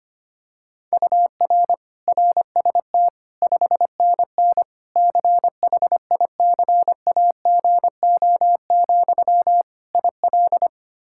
Morse code, heard as a keyed tone, about 25 wpm, 700 Hz. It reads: UR RST 5NN CHICAGO, IL
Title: K1USN SST QSO Practice 25wpm